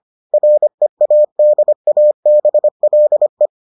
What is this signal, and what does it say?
Morse code, heard as a keyed tone, about 25 wpm, 600 Hz. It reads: READABLE